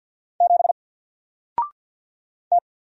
Morse code, heard as a keyed tone, about 50 wpm, 700 Hz. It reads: 7 T
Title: Single Letter-Number 50wpm